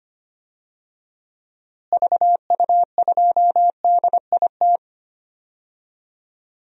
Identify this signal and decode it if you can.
Morse code, heard as a keyed tone, about 25 wpm, 700 Hz. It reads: VU2DIT
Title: DXCC-100 Call Signs - 6 Characters - Repeated 3x - 25wpm